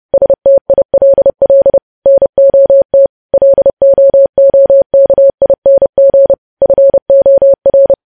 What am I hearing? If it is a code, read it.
Morse code, heard as a keyed tone, about 30 wpm, 550 Hz. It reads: STILL NOT LOOKING FOR
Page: news headlines in morse code at 30 wpm